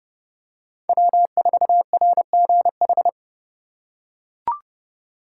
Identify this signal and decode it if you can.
Morse code, heard as a keyed tone, about 29 wpm, 700 Hz. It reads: W4RGH